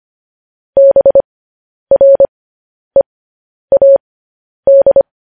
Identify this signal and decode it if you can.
Morse code, heard as a keyed tone, about 25 wpm, spaced as for 10 wpm, 550 Hz. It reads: BREAD
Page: news headlines in morse code at 10 wpm